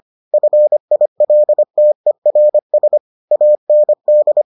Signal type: Morse code

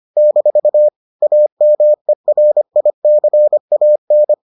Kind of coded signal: Morse code

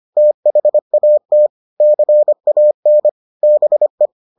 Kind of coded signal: Morse code

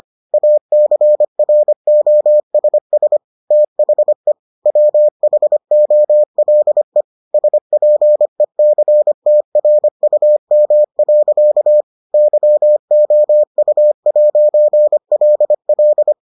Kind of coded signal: Morse code

